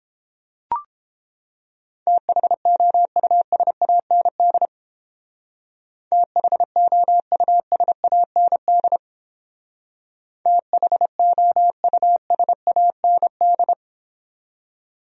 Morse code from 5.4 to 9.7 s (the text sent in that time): THOUSAND